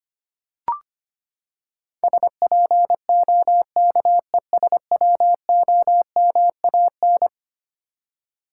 Morse code, heard as a keyed tone, about 25 wpm, 700 Hz. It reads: SPOKESWOMAN
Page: morse-code-ninja-practice-25wpm